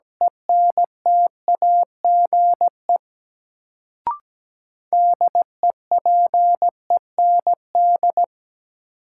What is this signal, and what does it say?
Morse code, heard as a keyed tone, about 17 wpm, 700 Hz. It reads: ENTAGE DEPEND
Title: Sentences from Top 1300 Words - New Word Drill Alt - 17wpm